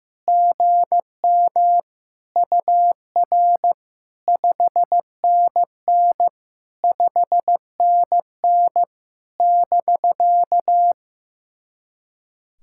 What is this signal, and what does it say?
Morse code, heard as a keyed tone, about 16 wpm, 700 Hz. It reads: GM UR 5NN 5NN <BK>
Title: SOTA QSO Practice 15wpm